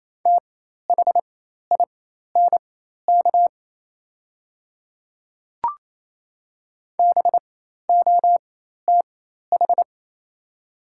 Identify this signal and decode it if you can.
Morse code, heard as a keyed tone, about 28 wpm, spaced as for 13 wpm, 690 Hz. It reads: THINK BOTH
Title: 4x Character Spacing - Top 300 Words - 28wpm